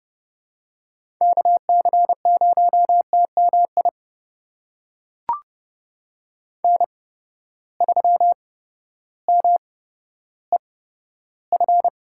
Morse code, heard as a keyed tone, about 30 wpm, spaced as for 14 wpm, 700 Hz. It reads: KC0TMI N3MEF
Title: USA Call Signs - 8x Character Spacing - 30wpm